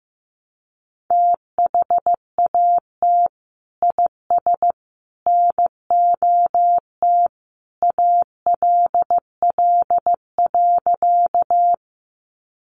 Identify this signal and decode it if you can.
Morse code, heard as a keyed tone, about 15 wpm, 700 Hz. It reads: THAT IS NOT ALL.
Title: Sentences from Top 600 Words 15wpm